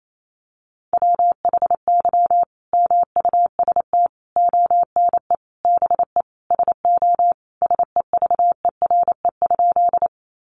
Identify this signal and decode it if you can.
Morse code, heard as a keyed tone, about 28 wpm, 700 Hz. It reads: WHY MUST ONE BE SO SEVERE?
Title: Sentences from Top 1400 Words 28wpm